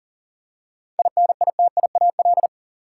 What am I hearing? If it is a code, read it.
Morse code, heard as a keyed tone, about 40 wpm, 700 Hz. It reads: INITIAL
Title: Seven Letter Words - 40wpm